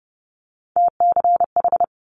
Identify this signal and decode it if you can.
Morse code, heard as a keyed tone, about 30 wpm, 700 Hz. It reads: TCH
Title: Multi-letter Phonemes Quick Fire 30wpm